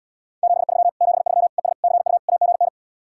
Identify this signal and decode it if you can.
Morse code, heard as a keyed tone, about 77 wpm, 700 Hz. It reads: 73 73 ES CU AGN